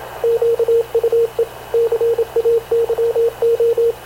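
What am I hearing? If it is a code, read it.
Morse code, heard as a keyed tone, about 27 wpm, 460 Hz, with radio noise. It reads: QUE CAYO
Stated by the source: Havana Radio CLA press (PX) broadcast, recorded by Radio Officer Paul Dumesnil aboard mv CNAV QUEST/CZDO.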